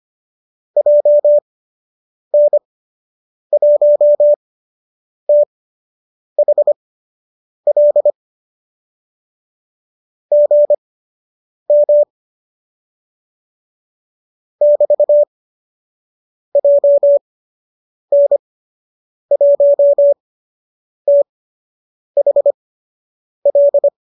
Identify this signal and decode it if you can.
Morse code, heard as a keyed tone, about 25 wpm, spaced as for 8 wpm, 570 Hz. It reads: JN1THL GM <BT> JN1THL
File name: qso003 08